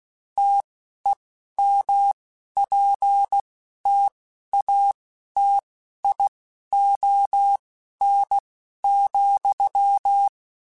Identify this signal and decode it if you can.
Morse code, heard as a keyed tone, about 16 wpm, spaced as for 11 wpm, 800 Hz. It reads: TEMPTATION,
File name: Spaced / Chuckles Spaced-mp3s / Temptation sp @15